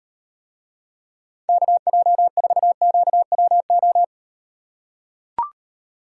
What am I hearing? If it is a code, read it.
Morse code, heard as a keyed tone, about 38 wpm, 700 Hz. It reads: KJ4QWO